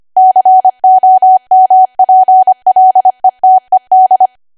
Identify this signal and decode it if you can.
Morse code, heard as a keyed tone, about 25 wpm, 750 Hz. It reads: COMPLETED